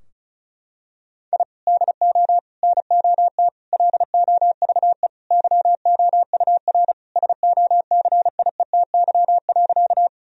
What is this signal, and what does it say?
Morse code, heard as a keyed tone, about 34 wpm, 700 Hz. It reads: I DO NOT LOVE YOUR SOCIETY.
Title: Sentences from Top 1000 Words - Review Part 4 - 35wpm